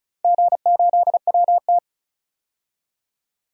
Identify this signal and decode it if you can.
Morse code, heard as a keyed tone, about 36 wpm, 700 Hz. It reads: G8WT
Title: DXCC-100 Call Signs - 4 Characters - 35wpm